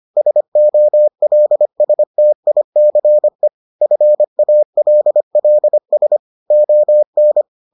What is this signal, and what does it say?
Morse code, heard as a keyed tone, about 25 wpm, 600 Hz. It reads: SOLSTICE FALLS ON